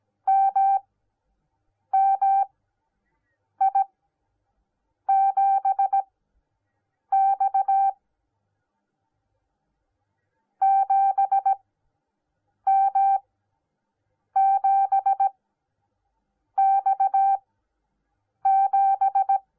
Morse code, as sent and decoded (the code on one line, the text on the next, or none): -- -- .. --... -..- / --... -- --... -..- --...
MMI7X 7M7X7